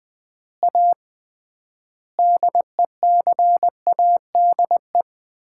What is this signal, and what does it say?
Morse code, heard as a keyed tone, about 20 wpm, 700 Hz. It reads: A DECADE
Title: 3x Word-Spacing - Sets of 5 Words - 20wpm